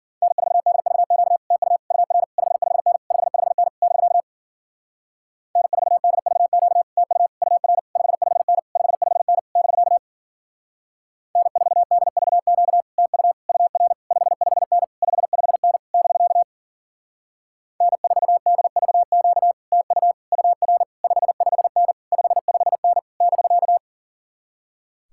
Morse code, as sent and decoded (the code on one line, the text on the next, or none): -. ....- -.. ...- --.- / - ..- / ..- .-. / ..... ..... -. / ..... ..... -. / -...-.- / -. ....- -.. ...- --.- / - ..- / ..- .-. / ..... ..... -. / ..... ..... -. / -...-.- / -. ....- -.. ...- --.- / - ..- / ..- .-. / ..... ..... -. / ..... ..... -. / -...-.- / -. ....- -.. ...- --.- / - ..- / ..- .-. / ..... ..... -. / ..... ..... -. / -...-.-
N4DVQ TU UR 55N 55N <BK> N4DVQ TU UR 55N 55N <BK> N4DVQ TU UR 55N 55N <BK> N4DVQ TU UR 55N 55N <BK>